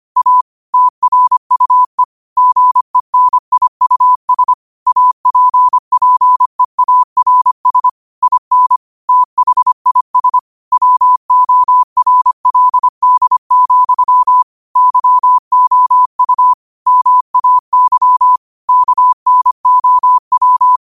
Morse code, heard as a keyed tone, about 25 wpm, 1000 Hz. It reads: A TRUE GENIUS APPEARS IN THIS WORLD, YOU MAY KNOW
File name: 25wpm